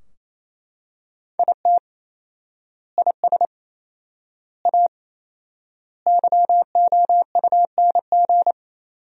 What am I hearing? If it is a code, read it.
Morse code, heard as a keyed tone, about 28 wpm, 700 Hz. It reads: IT IS A YOUNG
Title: 4x Word-Spacing - Sentences from Top 400 Words - 28wpm